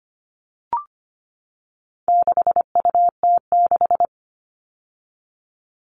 Morse code, heard as a keyed tone, about 25 wpm, 700 Hz. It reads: 6UT6
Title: All Cut Numbers - 4-Digit 25wpm